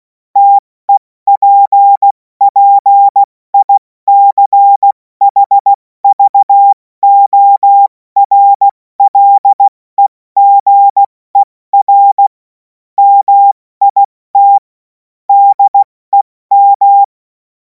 Morse code, heard as a keyed tone, about 16 wpm, 800 Hz. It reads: TEPPICHVORLEGER MIT DEM
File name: Morsekurs / Witze / Witz 10. März 2026 70 F80